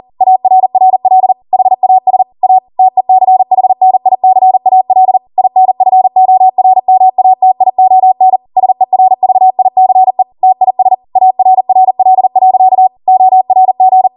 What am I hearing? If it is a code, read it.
Morse code, heard as a keyed tone, about 40 wpm, 750 Hz. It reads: ARRL HAS A TECHNICAL INFORMATION SERVICE TIS ARRL.ORG
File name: Morse / 40WPM